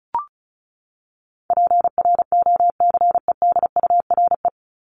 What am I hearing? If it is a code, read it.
Morse code, heard as a keyed tone, about 34 wpm, 700 Hz. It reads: PROCEDURE
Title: Top 1751-2000 Words 35wpm